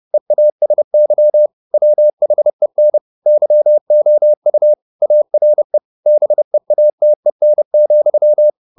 Morse code, heard as a keyed tone, about 30 wpm, 600 Hz. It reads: EASY WHEN YOU ARE BEATEN,